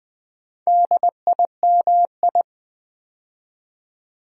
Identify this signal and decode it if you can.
Morse code, heard as a keyed tone, about 20 wpm, 700 Hz. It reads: DIMI
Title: Build A Word - 2 Letters At A Time - 8-Letter Words - 20wpm